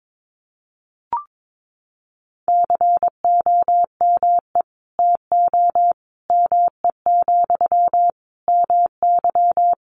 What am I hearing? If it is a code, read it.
Morse code, heard as a keyed tone, about 22 wpm, 700 Hz. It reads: COME TO ME, MY
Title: Sentences from Top 1500 Words 22wpm